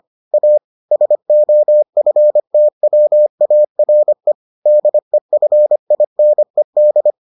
Morse code, heard as a keyed tone, about 25 wpm, 600 Hz. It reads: A SOFTWARE DEFINED